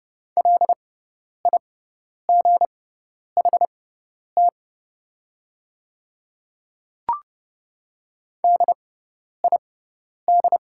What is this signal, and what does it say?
Morse code, heard as a keyed tone, about 30 wpm, spaced as for 10 wpm, 690 Hz. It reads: LIGHT DID
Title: 6x Character Spacing - Top 200 Words - 30wpm